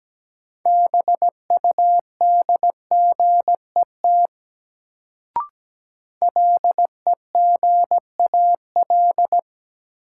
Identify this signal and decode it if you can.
Morse code, heard as a keyed tone, about 17 wpm, 700 Hz. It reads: BUDGET LEGAL
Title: Sentences from Top 1200 Words - New Word Drill - 17wpm